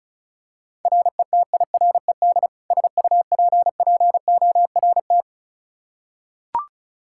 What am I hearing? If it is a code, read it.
Morse code, heard as a keyed tone, about 36 wpm, 700 Hz. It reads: RETIRED SUPPORT
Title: K1USN SST QSO Practice 35wpm